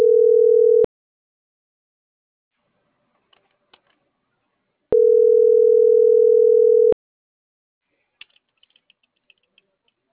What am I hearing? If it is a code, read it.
Ringback tone (North America), 440 + 480 Hz.